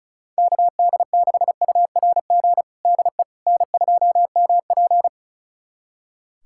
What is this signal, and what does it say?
Morse code, heard as a keyed tone, about 34 wpm, 700 Hz. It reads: KD6URG DE N2MP
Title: USA Callsign DE Callsign 35wpm